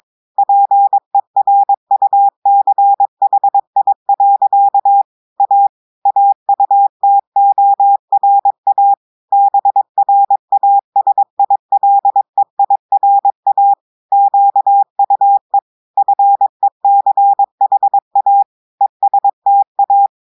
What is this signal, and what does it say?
Morse code, heard as a keyed tone, about 22 wpm, 800 Hz. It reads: PERUCHI. A AUTORA BRASILEIRA QUE FECHA ESTA
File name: audios / portugais / portugais texte en clair 1 vitesse 22 mots minutes